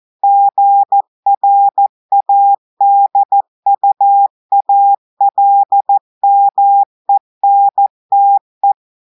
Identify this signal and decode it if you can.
Morse code, heard as a keyed tone, about 14 wpm, 800 Hz. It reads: GRADUALMENTE
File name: audios / portugais / portugais texte en clair 2 vitesse 14 mots minutes